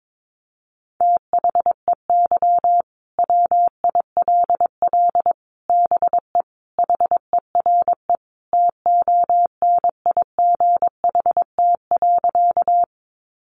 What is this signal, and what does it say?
Morse code, heard as a keyed tone, about 22 wpm, 700 Hz. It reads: THEY WILL BE HERE TONIGHT.
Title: Sentences from Top 600 Words 22wpm